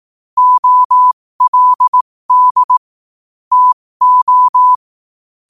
Morse code, heard as a keyed tone, about 18 wpm, spaced as for 15 wpm, 1000 Hz. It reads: OLD TO
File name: qotdfiles / 15wpm